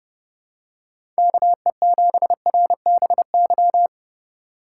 Morse code, heard as a keyed tone, about 30 wpm, 700 Hz. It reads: KE7RBY
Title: USA Call Signs - 6 Characters - 2x Character Spacing - 30wpm